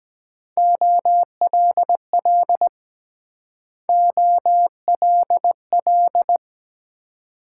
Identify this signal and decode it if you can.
Morse code, heard as a keyed tone, about 22 wpm, 700 Hz. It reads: OLL OLL